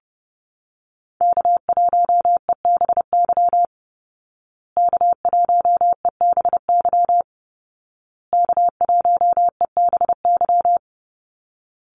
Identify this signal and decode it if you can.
Morse code, heard as a keyed tone, about 30 wpm, 700 Hz. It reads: K1EBY K1EBY K1EBY
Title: Call Signs Repeated 2x - 30wpm